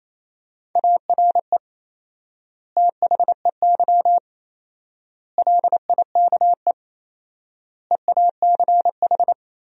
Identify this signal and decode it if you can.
Morse code, heard as a keyed tone, about 28 wpm, 700 Hz. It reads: ARE THEY LIKE EACH
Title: 4x Word-Spacing - Sentences from Top 200 Words - 28wpm